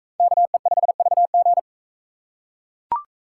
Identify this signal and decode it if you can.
Morse code, heard as a keyed tone, about 40 wpm, 700 Hz. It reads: KE5VG